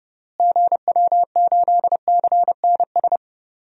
Morse code, heard as a keyed tone, about 30 wpm, 700 Hz. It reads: GW8CNS
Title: DXCC-100 Call Signs - 6 Characters - Repeated 2x - 30wpm